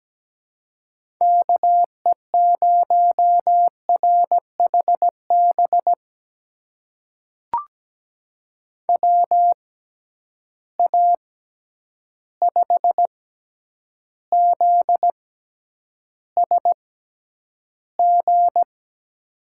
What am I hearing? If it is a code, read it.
Morse code, heard as a keyed tone, about 17 wpm, spaced as for 9 wpm, 700 Hz. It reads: KE0RHB WA5ZSG
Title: USA Call Signs - 6 Characters - 6x Character Spacing - 17wpm